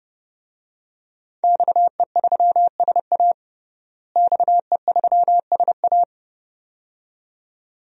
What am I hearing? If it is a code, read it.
Morse code, heard as a keyed tone, about 32 wpm, 700 Hz. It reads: XE3SA XE3SA
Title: DXCC-100 Call Signs - 5 Characters - Repeated 2x - 30wpm